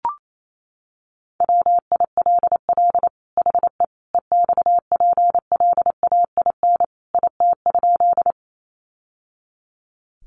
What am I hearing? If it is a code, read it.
Morse code, heard as a keyed tone, about 28 wpm, 700 Hz. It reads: WILL HE EXPLAIN IT?
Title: Sentences from Top 1100 Words 28wpm